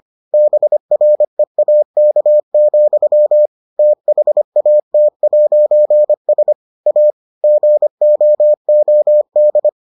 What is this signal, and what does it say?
Morse code, heard as a keyed tone, about 25 wpm, 600 Hz. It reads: BREAK, THAT'S A GOOD